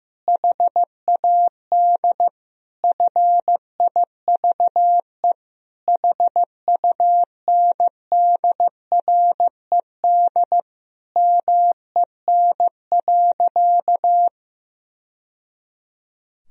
Morse code, heard as a keyed tone, about 15 wpm, 700 Hz. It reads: HAD FIVE HUNDRED MEN.
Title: Sentences from Top 1000 Words - Review Part 2 - 15wpm